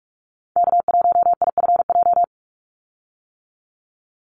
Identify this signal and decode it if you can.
Morse code, heard as a keyed tone, about 44 wpm, 700 Hz. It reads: K1IFJ
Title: Call Signs - 45wpm